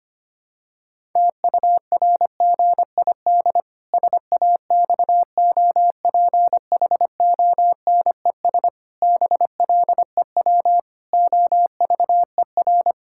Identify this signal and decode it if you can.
Morse code, heard as a keyed tone, about 25 wpm, 700 Hz. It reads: TURGID SAXOPHONES BLEW OVER
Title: Pangram Challenge v2 - 25wpm